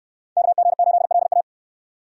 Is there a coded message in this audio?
Yes: Morse code from 0.4 to 1.4 s, heard as a keyed tone, about 69 wpm, 700 Hz. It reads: KG8LA